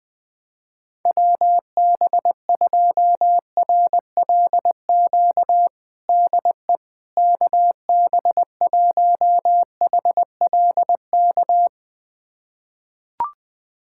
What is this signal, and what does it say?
Morse code, heard as a keyed tone, about 20 wpm, 700 Hz. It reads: WB2RLQ DE KB1HLK
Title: USA Callsign DE Callsign - 6 Characters - 20wpm